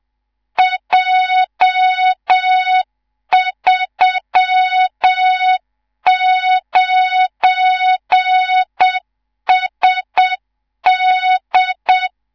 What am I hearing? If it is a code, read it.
Morse code, heard as a keyed tone, about 7 wpm, 3700 Hz, with noise. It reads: J39SD
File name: Morze-HA2MN / hang / 19 GYAKORLAT